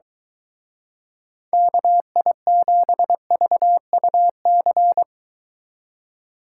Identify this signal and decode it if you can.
Morse code, heard as a keyed tone, about 23 wpm, 700 Hz. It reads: KI7VUC